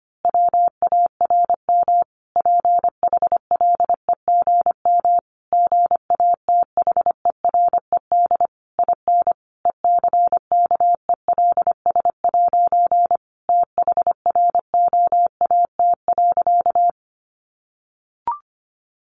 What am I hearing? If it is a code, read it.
Morse code, heard as a keyed tone, about 25 wpm, 700 Hz. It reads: WARM PHLEGM GATHERED IN ECKELS' THROAT.
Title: A Sound Of Thunder 25wpm